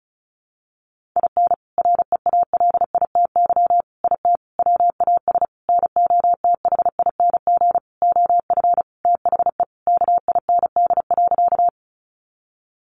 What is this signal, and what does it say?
Morse code, heard as a keyed tone, about 34 wpm, 700 Hz. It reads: IN REALITY IT WAS NOTHING OF THE KIND.
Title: Sentences from Top 1200 Words 35wpm